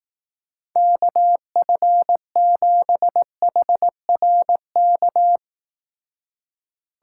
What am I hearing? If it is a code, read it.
Morse code, heard as a keyed tone, about 19 wpm, 700 Hz. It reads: KF7HRK
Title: Speed Racing - USA 2x3 Call Signs 15wpm